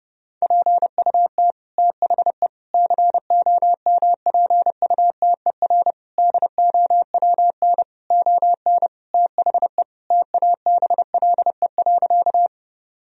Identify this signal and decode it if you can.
Morse code, heard as a keyed tone, about 30 wpm, 700 Hz. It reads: PUT THE COMPUTER DOWN ON THE TABLE.
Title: Sentences from Top 1000 Words - Review Part 3 - 30wpm